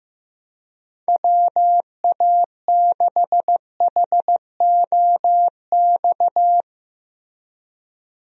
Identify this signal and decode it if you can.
Morse code, heard as a keyed tone, about 15 wpm, 700 Hz. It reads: WA6HOX
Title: USA Call Signs - 6 Characters - Repeated 2x - 15wpm